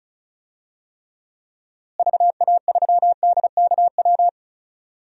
Morse code, heard as a keyed tone, about 36 wpm, 700 Hz. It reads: VA3DKW
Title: DXCC-100 Call Signs - 6 Characters - Repeated 3x - 35wpm